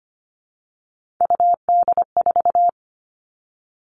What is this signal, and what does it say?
Morse code, heard as a keyed tone, about 25 wpm, 700 Hz. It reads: UD4
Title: All Cut Numbers - 3-Digit 25wpm